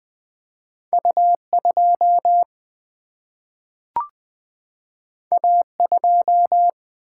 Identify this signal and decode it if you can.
Morse code, heard as a keyed tone, about 20 wpm, 700 Hz. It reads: U 2 A 2
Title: Single-Letter Single-Number 20wpm